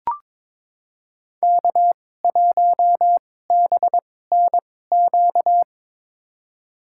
Morse code, heard as a keyed tone, about 22 wpm, spaced as for 16 wpm, 700 Hz. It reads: K1BNQ
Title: USA Call Signs - 5 Characters - 2x Character Spacing - 22wpm